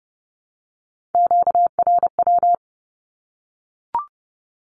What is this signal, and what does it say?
Morse code, heard as a keyed tone, about 30 wpm, 700 Hz. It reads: QRW
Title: Q-Codes Repeated 3x 30wpm